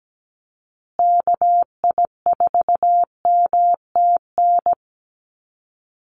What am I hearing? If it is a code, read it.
Morse code, heard as a keyed tone, about 17 wpm, 700 Hz. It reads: KI4MTN
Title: USA Call Signs Repeated 2x 17wpm